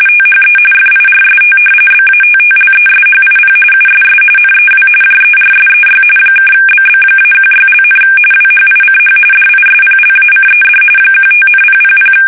RTTY, about 45.45 baud. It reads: THE FIRST BATTLESHIP USS NEW JERSEY (BB-16), WHICH DISPLACED 15,000
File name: fsk850 / battleshipnj